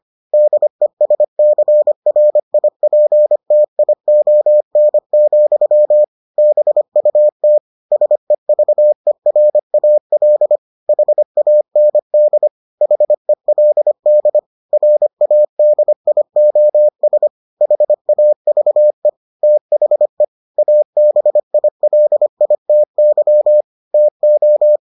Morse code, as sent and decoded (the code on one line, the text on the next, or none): -.. ... -.-. .-. .. .--. - .. --- -. --..-- / -... ..- - / ... ...- . .-. .- .-.. / .... .- -. -.. / .... .-.. -.. / .-. .- -.. .. --- ... / .... .- ...- . / - .... / .- -... .. .-.. .. - -.-- / - ---
DESCRIPTION, BUT SEVERAL HAND HELD RADIOS HAVE THE ABILITY TO